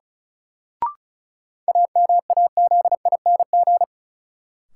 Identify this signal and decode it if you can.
Morse code, heard as a keyed tone, about 36 wpm, 700 Hz. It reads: AMAZING
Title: Sentences from Top 1500 Words - New Word Drill - 35wpm